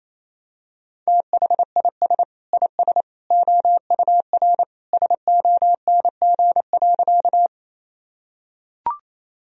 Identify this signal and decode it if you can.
Morse code, heard as a keyed tone, about 28 wpm, 690 Hz. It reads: THIS IS OUR SONG.
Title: Sentences from Top 1000 Words - Review Part 2 - 28wpm